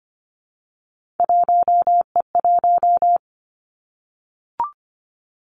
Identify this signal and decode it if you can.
Morse code, heard as a keyed tone, about 25 wpm, 700 Hz. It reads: 1E1
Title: Three Challenging Letter-Number Combinations 25wpm